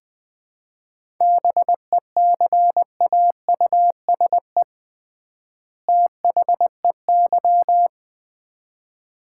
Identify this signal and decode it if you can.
Morse code, heard as a keyed tone, about 21 wpm, 700 Hz. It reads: BECAUSE THEY
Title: 3x Word-Spacing - Sets of 2 Words - 20wpm